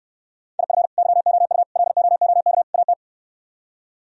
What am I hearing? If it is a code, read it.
Morse code, heard as a keyed tone, about 68 wpm, 700 Hz. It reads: IF YOU LOOK AT